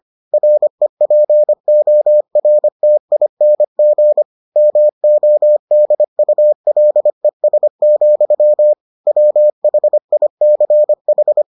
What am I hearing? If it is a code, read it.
Morse code, heard as a keyed tone, about 25 wpm, 600 Hz. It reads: REPORTING MODULES, WHICH